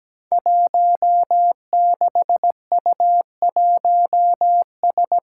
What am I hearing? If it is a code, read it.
Morse code, heard as a keyed tone, about 17 wpm, 700 Hz. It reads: 16U1S
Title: Five Challenging Letter-Number Combinations 17wpm